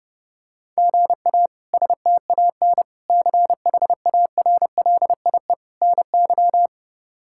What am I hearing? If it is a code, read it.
Morse code, heard as a keyed tone, about 30 wpm, 700 Hz. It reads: GA STAN CHARLIE NY
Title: K1USN SST QSO Practice - 30wpm